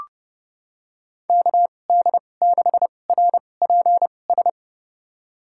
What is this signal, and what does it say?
Morse code, heard as a keyed tone, about 31 wpm, spaced as for 22 wpm, 700 Hz. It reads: KD6RPS
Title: USA Call Signs - 6 Characters - 2x Character Spacing - 30wpm